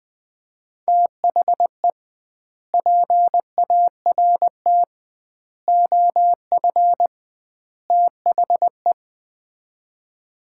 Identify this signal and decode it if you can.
Morse code, heard as a keyed tone, about 21 wpm, 700 Hz. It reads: THE PART OF THE
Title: 2x Word-Spacing - Sets of 4 Words - 20wpm